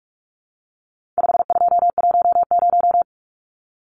Morse code, heard as a keyed tone, about 44 wpm, 700 Hz. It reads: 5210
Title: Numbers - 4-Digit 45wpm